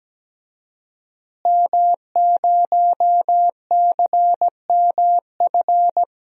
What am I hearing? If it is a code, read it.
Morse code, heard as a keyed tone, about 17 wpm, 700 Hz. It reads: M0CMF
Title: DXCC-100 Call Signs - 5 Characters - Repeated 2x - 17wpm